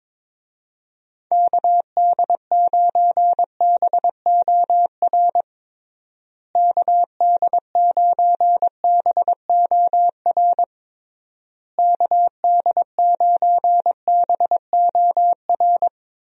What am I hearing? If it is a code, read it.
Morse code, heard as a keyed tone, about 22 wpm, 700 Hz. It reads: KD9BOR KD9BOR KD9BOR
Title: USA Call Signs - 6 Characters - Repeated 3x - 22wpm